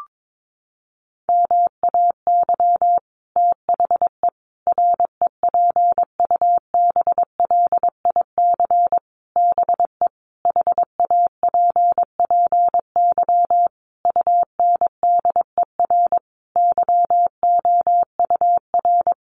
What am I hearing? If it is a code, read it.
Morse code, heard as a keyed tone, about 22 wpm, 700 Hz. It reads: MAY THE REPUBLIC BE HAPPY UNDER YOUR
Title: Sentences from Top 1500 Words 22wpm